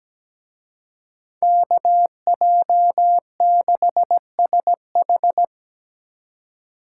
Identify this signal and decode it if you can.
Morse code, heard as a keyed tone, about 20 wpm, 700 Hz. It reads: KJ6SH
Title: Speed Racing - USA 2x2 Call Signs 17wpm